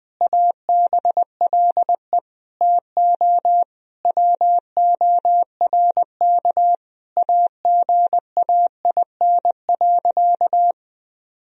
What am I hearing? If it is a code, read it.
Morse code, heard as a keyed tone, about 19 wpm, 700 Hz. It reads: ABLE TO WORK AGAIN.
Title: Sentences from Top 1000 Words - Review Part 2 - 20wpm